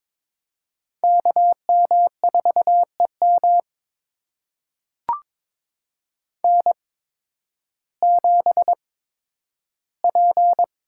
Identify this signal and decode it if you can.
Morse code, heard as a keyed tone, about 22 wpm, spaced as for 12 wpm, 700 Hz. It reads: KM4EM N7P
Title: USA Call Signs - 5 Characters - 8x Character Spacing - 22wpm